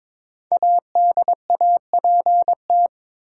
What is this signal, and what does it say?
Morse code, heard as a keyed tone, about 22 wpm, 700 Hz. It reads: ADAPT
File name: 5 Letter Words - First 3 Letters Given - 22wpm